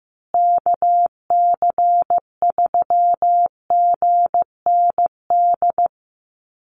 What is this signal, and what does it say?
Morse code, heard as a keyed tone, about 15 wpm, 700 Hz. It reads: KC3GND
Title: USA Call Signs Repeated 2x 15wpm